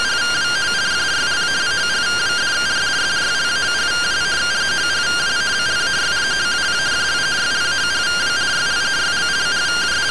Two-tone teleprinter signal, about 170 Hz shift.